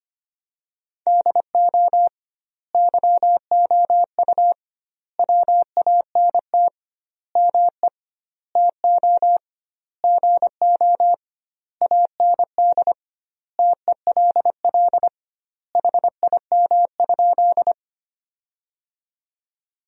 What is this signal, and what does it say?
Morse code, heard as a keyed tone, about 25 wpm, 700 Hz. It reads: DO YOU WANT ME TO GO AND TELL HIM?
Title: 2x Word-Spacing - Sentences from Top 200 Words - 25wpm